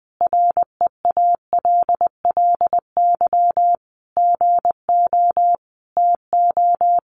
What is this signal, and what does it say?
Morse code, heard as a keyed tone, about 20 wpm, 700 Hz. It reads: REALLY GO TO
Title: Sentences from Top 600 Words 20wpm